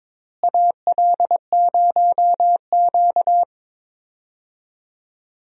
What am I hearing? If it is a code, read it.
Morse code, heard as a keyed tone, about 22 wpm, 700 Hz. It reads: AL0Q
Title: USA Call Signs - 4 Characters - Repeated 3x - 22wpm